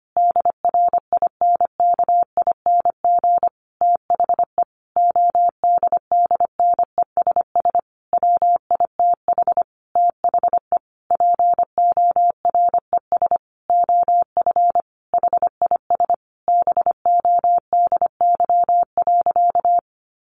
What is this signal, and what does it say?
Morse code, heard as a keyed tone, about 25 wpm, 700 Hz. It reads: DRINKING THE ODDNESS WITH THE PORES OF HIS BODY.
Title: A Sound Of Thunder 25wpm